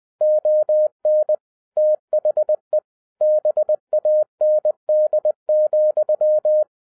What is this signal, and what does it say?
Morse code, heard as a keyed tone, about 20 wpm, 610 Hz. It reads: ON THE BAND,
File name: 2012-01-mp3 / n16-THROUGH EIGHT MONTHS-20wpm